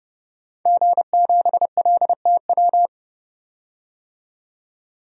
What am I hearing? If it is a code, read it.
Morse code, heard as a keyed tone, about 32 wpm, 700 Hz. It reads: G7LTW
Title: DXCC-100 Call Signs - 5 Characters - 30wpm